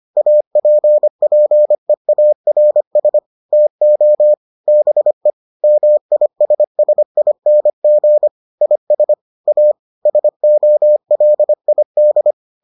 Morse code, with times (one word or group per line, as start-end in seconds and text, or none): 0.2-3.2 APPEARS
3.5-4.3 TO
4.7-5.3 BE
5.6-8.3 MISSING
8.6-9.1 IS
9.5-9.7 A
10.1-12.3 SOLID